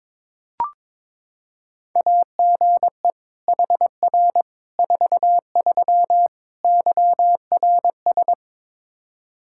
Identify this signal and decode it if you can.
Morse code, heard as a keyed tone, about 22 wpm, 700 Hz. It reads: AGE HR 43 YRS
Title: Practice QSO - Exchange 3 - More Info - 22wpm